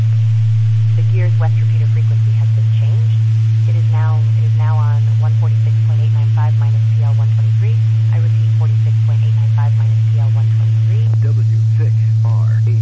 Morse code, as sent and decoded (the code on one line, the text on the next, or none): none